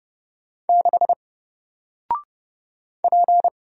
Morse code, heard as a keyed tone, about 30 wpm, 700 Hz. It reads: 6 P
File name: Single-Letter-Number-30wpm